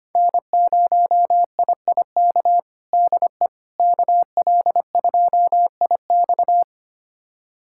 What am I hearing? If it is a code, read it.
Morse code, heard as a keyed tone, about 25 wpm, 700 Hz. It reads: N0IIK DE KL2IX
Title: USA Callsign DE Callsign - 5 Characters - 25wpm